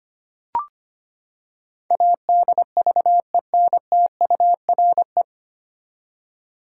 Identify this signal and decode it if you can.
Morse code, heard as a keyed tone, about 25 wpm, 700 Hz. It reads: ADVENTURE
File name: Nine Letter Words - 25wpm